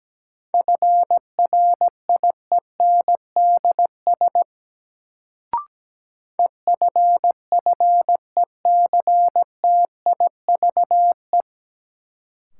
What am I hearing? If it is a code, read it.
Morse code, heard as a keyed tone, about 17 wpm, 700 Hz. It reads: FRIENDS EFFECTIVE
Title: Sentences from Top 1600 Words - New Word Drill Alt - 17wpm